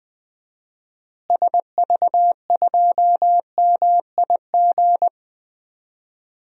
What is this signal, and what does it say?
Morse code, heard as a keyed tone, about 20 wpm, 700 Hz. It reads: SV2MIG
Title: DXCC-100 Call Signs - 6 Characters - Repeated 3x - 20wpm